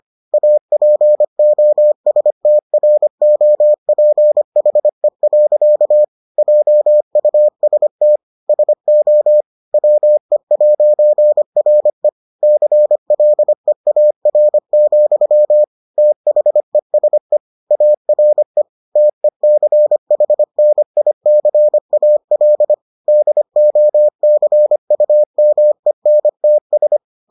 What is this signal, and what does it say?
Morse code, heard as a keyed tone, about 25 wpm, 600 Hz. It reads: APOSTROPHE. JUST SO WE'RE CLEAR, THESE ARE TECHNICAL DOCUMENTS